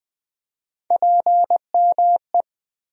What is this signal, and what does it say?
Morse code, heard as a keyed tone, about 20 wpm, 700 Hz. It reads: PME
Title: USA Call Sign Postfixes - 3 Letters 20wpm